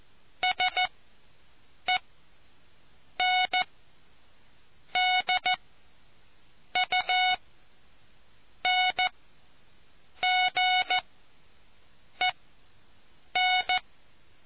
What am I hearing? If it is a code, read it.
Morse code, heard as a keyed tone, about 14 wpm, spaced as for 6 wpm, 740 Hz. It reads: SENDUNGEN